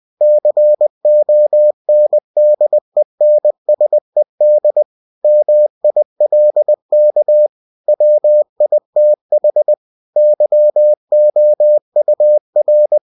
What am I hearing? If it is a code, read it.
Morse code, heard as a keyed tone, about 20 wpm, 600 Hz. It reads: CONDENSED MILK WITH YOUR
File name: cw-books / winnie-the-pooh / winnie-20wpm-page-0015